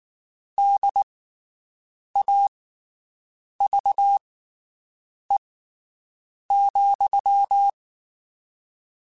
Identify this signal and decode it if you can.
Morse code, heard as a keyed tone, about 19 wpm, spaced as for 7 wpm, 800 Hz. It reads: DAVE,
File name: (spaced) / QSOs Spaced-mp3s / WD0BGP 1 @18 sp